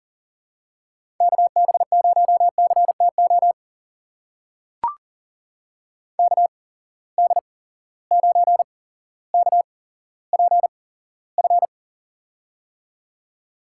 Morse code, heard as a keyed tone, about 40 wpm, spaced as for 17 wpm, 700 Hz. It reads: KB0CTO KD9KPF
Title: USA Call Signs - 6 Characters - 8x Character Spacing - 40wpm